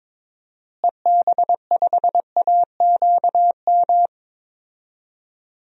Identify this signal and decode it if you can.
Morse code, heard as a keyed tone, about 21 wpm, 700 Hz. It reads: EB5AQM
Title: DXCC-100 Call Signs - 6 Characters - Repeated 3x - 22wpm